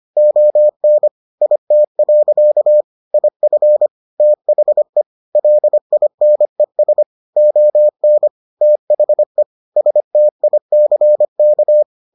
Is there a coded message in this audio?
Yes: Morse code from 0.2 to 11.8 s, heard as a keyed tone, about 25 wpm, 600 Hz. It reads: ON IT. IF THE LINES ON THE STICK